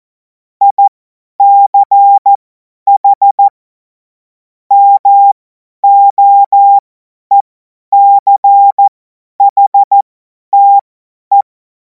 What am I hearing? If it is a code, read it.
Morse code, heard as a keyed tone, about 14 wpm, spaced as for 10 wpm, 800 Hz. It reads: ICH MOECHTE
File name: Morsekurs / Witze / Witz 04. August 2026 50 F70